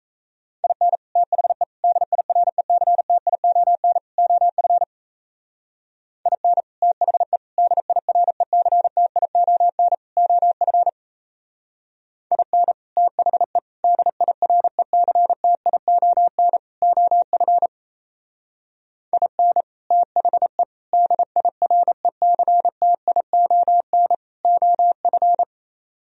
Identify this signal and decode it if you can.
Morse code, heard as a keyed tone, about 34 wpm, 700 Hz. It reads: IN THE DIRECTION OF IN THE DIRECTION OF IN THE DIRECTION OF IN THE DIRECTION OF
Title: Speed-Racing - Sets of 4 Words 28wpm